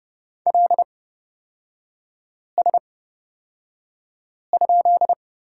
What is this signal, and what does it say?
Morse code, heard as a keyed tone, about 30 wpm, spaced as for 5 wpm, 700 Hz. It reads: LS?
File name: Lesson 36e - BT - Three Characters 30wpm